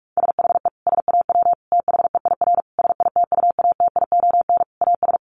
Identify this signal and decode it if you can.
Morse code, heard as a keyed tone, about 44 wpm, 700 Hz. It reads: SHE SAW THEIR SITUATION AS
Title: Sentences from Top 1200 Words 45wpm